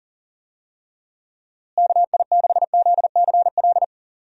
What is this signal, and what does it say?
Morse code, heard as a keyed tone, about 40 wpm, 700 Hz. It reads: KI6ZCL
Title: USA Call Signs - 6 Characters - Repeated 2x - 40wpm